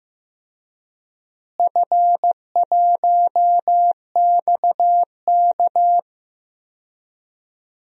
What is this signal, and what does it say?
Morse code, heard as a keyed tone, about 15 wpm, 700 Hz. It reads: F1XK
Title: DXCC-100 Call Signs - 4 Characters - Repeated 2x - 15wpm